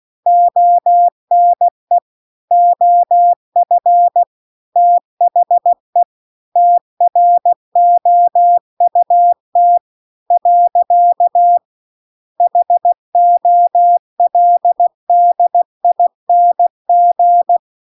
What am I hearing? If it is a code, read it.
Morse code, heard as a keyed tone, about 16 wpm, 700 Hz. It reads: ONE OF THE TROUT . HOLDING